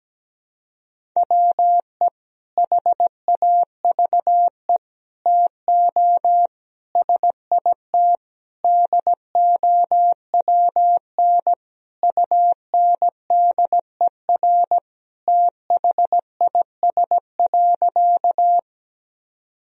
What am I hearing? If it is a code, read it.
Morse code, heard as a keyed tone, about 17 wpm, 700 Hz. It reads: WE HAVE TO SIT DOWN UNDER THIS.
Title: Sentences from Top 1000 Words - Review Part 1 - 17wpm